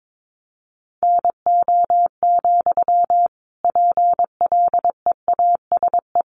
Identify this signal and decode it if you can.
Morse code, heard as a keyed tone, about 22 wpm, 700 Hz. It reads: NO, PLEASE
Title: Sentences from Top 1500 Words 22wpm